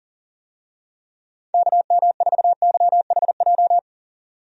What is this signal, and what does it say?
Morse code, heard as a keyed tone, about 40 wpm, 700 Hz. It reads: KM4YHJ
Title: USA Call Signs - 6 Characters - 40wpm